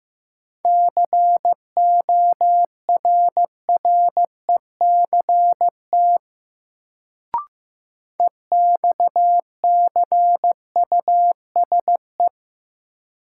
Morse code, heard as a keyed tone, about 15 wpm, 700 Hz. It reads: CORRECT EXCUSE
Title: Sentences from Top 700 Words - New Word Drill - 15wpm